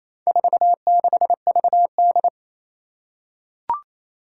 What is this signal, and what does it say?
Morse code, heard as a keyed tone, about 27 wpm, 700 Hz. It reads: 46VD
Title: Four Challenging Letter-Number Combinations 28wpm